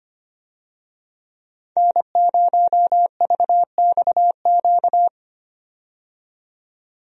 Morse code, heard as a keyed tone, about 25 wpm, 700 Hz. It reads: N0VXQ
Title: USA Call Signs - 5 Characters - 25wpm